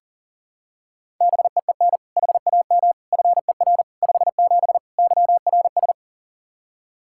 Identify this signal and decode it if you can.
Morse code, heard as a keyed tone, about 40 wpm, 700 Hz. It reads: BEEN HAM FER 57 YRS
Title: K1USN SST QSO Practice 40wpm